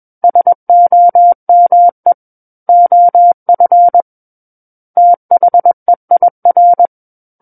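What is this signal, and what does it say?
Morse code, heard as a keyed tone, about 21 wpm, 700 Hz. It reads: SOME OF THEIR
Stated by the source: Morse code practice reading of a book chapter at 21 words per minute.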